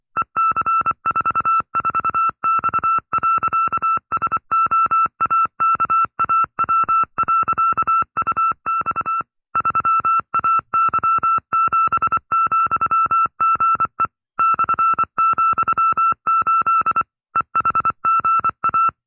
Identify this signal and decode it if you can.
Morse code, heard as a keyed tone, about 25 wpm, 1330 Hz. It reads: EC44X.SOAKAW.UX 3AY7,GE /,8 EHGA